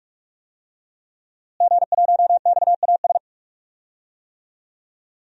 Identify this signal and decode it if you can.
Morse code, heard as a keyed tone, about 45 wpm, 700 Hz. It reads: G1XAS
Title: DXCC-100 Call Signs - 5 Characters - Repeated 3x - 45wpm